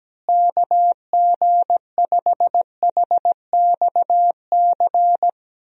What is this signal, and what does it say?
Morse code, heard as a keyed tone, about 17 wpm, 700 Hz. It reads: KG5HXC
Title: USA Call Signs - 6 Characters - Repeated 3x - 17wpm